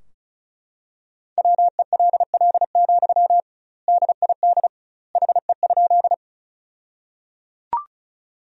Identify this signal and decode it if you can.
Morse code, heard as a keyed tone, about 34 wpm, 700 Hz. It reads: WELL, DID HE?
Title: 2x Word-Spacing - Sentences from Top 200 Words - 35wpm